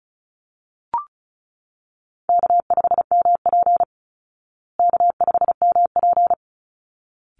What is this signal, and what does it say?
Morse code, heard as a keyed tone, about 34 wpm, 700 Hz. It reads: K5MP K5MP
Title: Call Signs Repeated 3x - 35wpm